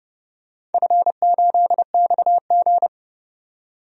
Morse code, heard as a keyed tone, about 30 wpm, 700 Hz. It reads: F8XG
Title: DXCC-100 Call Signs - 4 Characters - 30wpm